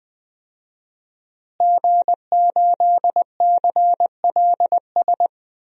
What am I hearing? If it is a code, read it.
Morse code, heard as a keyed tone, about 20 wpm, 700 Hz. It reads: G8CLS
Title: DXCC-100 Call Signs - 5 Characters - Repeated 3x - 20wpm